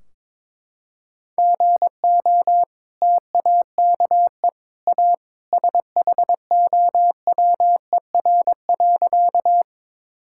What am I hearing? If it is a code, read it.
Morse code, heard as a keyed tone, about 22 wpm, 700 Hz. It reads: GO TAKE A SHOWER.
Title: Sentences from Top 1600 Words 22wpm